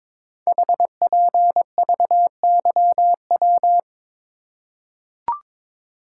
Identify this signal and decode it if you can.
Morse code, heard as a keyed tone, about 22 wpm, 700 Hz. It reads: HPVYW
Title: Code Groups - 5 Letters - 22wpm